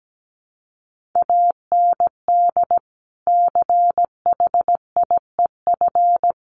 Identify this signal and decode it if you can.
Morse code, heard as a keyed tone, about 17 wpm, 700 Hz. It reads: AND CHIEF
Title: Sets of 5 Words Encore2 17wpm